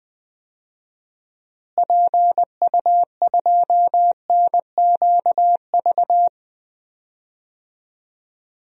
Morse code, heard as a keyed tone, about 20 wpm, 700 Hz. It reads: PU2NQV
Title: DXCC-100 Call Signs - 6 Characters - Repeated 3x - 20wpm